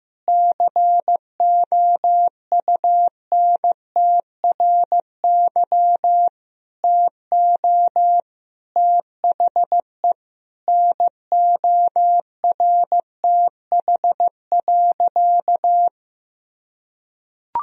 Morse code, heard as a keyed tone, about 15 wpm, 700 Hz. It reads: COUNTRY TO THE NORTH.
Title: Sentences from Top 1000 Words - Review Part 1 - 15wpm